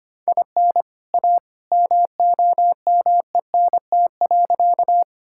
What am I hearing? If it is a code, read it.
Morse code, heard as a keyed tone, about 25 wpm, 700 Hz. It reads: IN A MOMENT.
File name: practice / 25wpm / Sentences with 2-Word Phrases - 25wpm